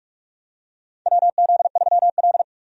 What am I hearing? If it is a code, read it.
Morse code, heard as a keyed tone, about 46 wpm, 700 Hz. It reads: WZ3L
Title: USA Call Signs - 4 Characters - 45wpm